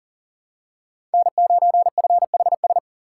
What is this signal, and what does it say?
Morse code, heard as a keyed tone, about 40 wpm, 700 Hz. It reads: N9FHS
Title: USA Call Signs - 5 Characters - 3x Character Spacing - 40wpm